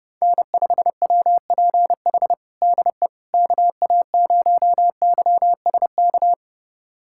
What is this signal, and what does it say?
Morse code, heard as a keyed tone, about 32 wpm, 700 Hz. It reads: N5WPH DE KA0YSK
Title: USA Callsign DE Callsign 30wpm